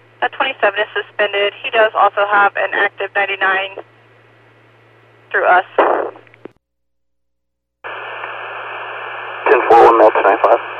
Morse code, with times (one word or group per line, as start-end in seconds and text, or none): none